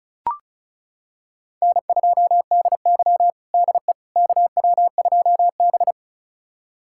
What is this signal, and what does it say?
Morse code, heard as a keyed tone, about 34 wpm, 700 Hz. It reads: N2DY DE KW2B
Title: USA Callsign DE Callsign - 4 Characters - 35wpm